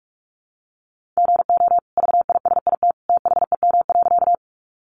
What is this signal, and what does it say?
Morse code, heard as a keyed tone, about 44 wpm, 700 Hz. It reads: GO VISIT THEM.
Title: Sentences from Top 900 Words 45wpm